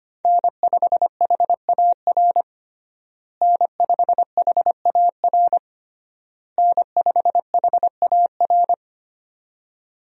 Morse code, heard as a keyed tone, about 25 wpm, 700 Hz. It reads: N5HAR N5HAR N5HAR
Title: USA Call Signs - 5 Characters - Repeated 3x - 25wpm